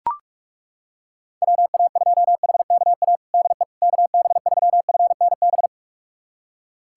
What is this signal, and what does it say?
Morse code, heard as a keyed tone, about 41 wpm, 700 Hz. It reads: WA2HKA DE KB3FNB
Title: USA Callsign DE Callsign - 6 Characters - 45wpm